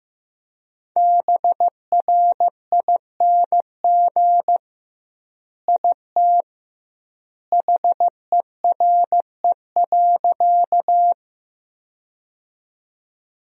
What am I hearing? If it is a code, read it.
Morse code, heard as a keyed tone, about 16 wpm, 700 Hz. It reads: BRING IT HERE.
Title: 2x Word-Spacing - Sentences from Top 500 Words - 15wpm